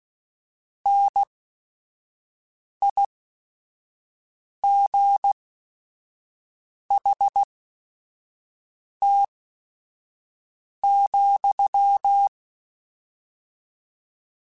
Morse code, heard as a keyed tone, about 16 wpm, spaced as for 5 wpm, 800 Hz. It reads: NIGHT,
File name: Weather Sets / Weather Sets-mp3s / Weather 4 17 @5